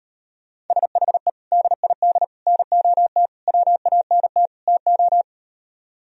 Morse code, heard as a keyed tone, about 38 wpm, 700 Hz. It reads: SHE DID NOT WANT TO